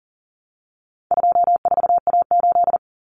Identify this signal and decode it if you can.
Morse code, heard as a keyed tone, about 40 wpm, 700 Hz. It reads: 24A8
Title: Common Cut Numbers - 4-Digit 40wpm